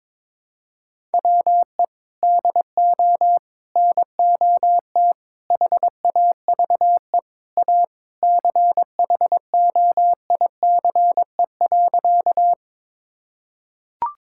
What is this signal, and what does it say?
Morse code, heard as a keyed tone, about 22 wpm, 700 Hz. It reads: WE DO NOT HAVE A CHOICE.
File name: practice / 22wpm / Sentences with 3-Word Phrases - 22wpm